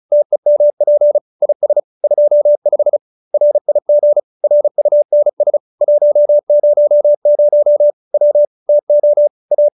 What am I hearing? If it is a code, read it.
Morse code, heard as a keyed tone, about 34 wpm, 590 Hz. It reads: TEMP IS 25 RIG RUNS 100 W TO A
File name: QSO302 35